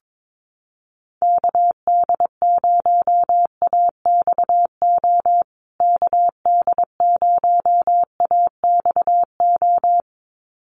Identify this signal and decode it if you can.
Morse code, heard as a keyed tone, about 22 wpm, 700 Hz. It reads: KD0AXO KD0AXO
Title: USA Call Signs Repeated 2x 22wpm